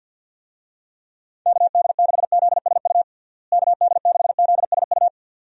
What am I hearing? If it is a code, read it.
Morse code, heard as a keyed tone, about 52 wpm, 700 Hz. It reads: KD6ZSU KD6ZSU
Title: USA Call Signs - 6 Characters - Repeated 2x - 50wpm